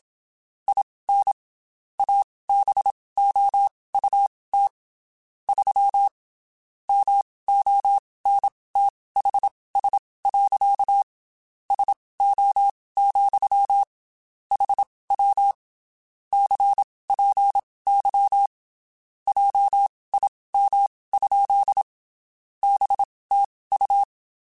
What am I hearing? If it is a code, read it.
Morse code, heard as a keyed tone, about 27 wpm, spaced as for 19 wpm, 800 Hz. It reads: IN ABOUT 3 MONTHS. SO, HW CPY JIM? BTU
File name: QSOs / QSOs-mp3s / WD0BGP 1@25